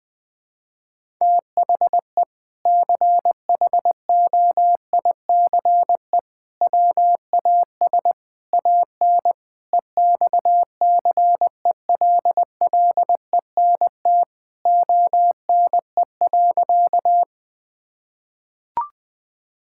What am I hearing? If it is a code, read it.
Morse code, heard as a keyed tone, about 21 wpm, 700 Hz. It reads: THE CHOICE WAS AN EXCELLENT ONE.
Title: Sentences from Top 1000 Words - Review Part 6 - 20wpm